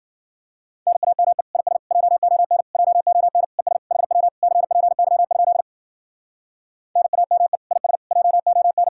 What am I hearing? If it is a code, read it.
Morse code, heard as a keyed tone, about 52 wpm, 700 Hz. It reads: NAME IS JON JON ES HW CPY? NAME IS JON